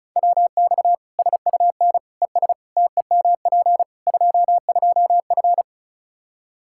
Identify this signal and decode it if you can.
Morse code, heard as a keyed tone, about 36 wpm, 700 Hz. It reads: WX SUN ES TEMP 22F
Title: Practice QSO - Exchange 2 - Info - 35wpm